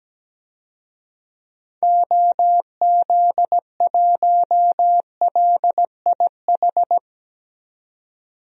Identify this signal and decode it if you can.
Morse code, heard as a keyed tone, about 17 wpm, 700 Hz. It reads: OZ1LIH
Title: DXCC-100 Call Signs - 6 Characters - 17wpm